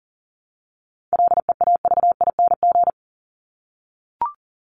Morse code, heard as a keyed tone, about 40 wpm, 700 Hz. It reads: LEAVING
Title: ING Suffix - Top 100 Words with Base Words 40wpm